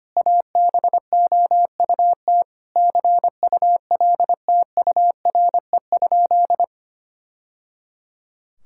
Morse code, heard as a keyed tone, about 25 wpm, 700 Hz. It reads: ABOUT CULTURE?
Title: Sentences from Top 1000 Words - Review Part 6 - 25wpm